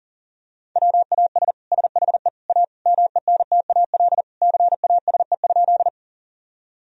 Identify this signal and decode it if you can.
Morse code, heard as a keyed tone, about 40 wpm, 700 Hz. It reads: WAS SHE A MENTAL CASE?
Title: Sentences from Top 1000 Words - Review Part 4 - 40wpm